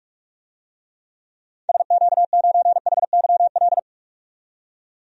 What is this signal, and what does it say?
Morse code, heard as a keyed tone, about 46 wpm, 700 Hz. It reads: SQ9HYL
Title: DXCC-100 Call Signs - 6 Characters - 45wpm